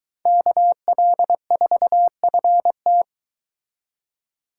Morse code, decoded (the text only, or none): KL4FT